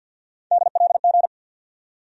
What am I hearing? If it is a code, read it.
Morse code, heard as a keyed tone, about 53 wpm, 700 Hz. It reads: DLG
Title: USA Call Sign Postfixes - 3 Letters 50wpm